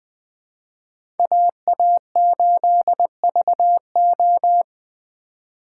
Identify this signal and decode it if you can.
Morse code, heard as a keyed tone, about 20 wpm, 700 Hz. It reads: AA8VO
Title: USA Call Signs - 5 Characters - 3x Character Spacing - 20wpm